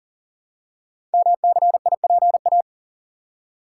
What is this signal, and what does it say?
Morse code, heard as a keyed tone, about 41 wpm, 690 Hz. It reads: MCIPA
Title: Code Groups - 5 Letters - 40wpm